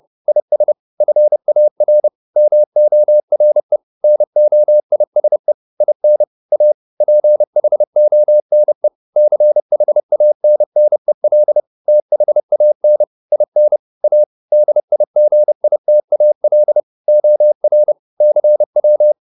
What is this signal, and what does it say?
Morse code, heard as a keyed tone, about 30 wpm, 600 Hz. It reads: IS FAR MORE NOISE IN A PHONE CHANNEL THAN IN A DIGITAL OR CW